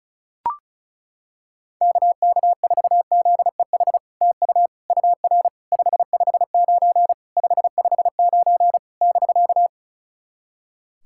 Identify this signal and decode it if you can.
Morse code, heard as a keyed tone, about 34 wpm, 700 Hz. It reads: KK4ZEH TU UR 559 559 <BK>
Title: SOTA QSO Practice 35wpm